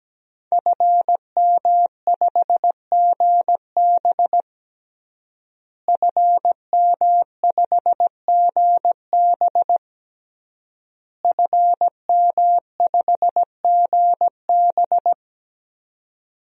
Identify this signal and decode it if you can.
Morse code, heard as a keyed tone, about 17 wpm, 700 Hz. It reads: FM5GB FM5GB FM5GB
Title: DXCC-100 Call Signs - 5 Characters - Repeated 3x - 17wpm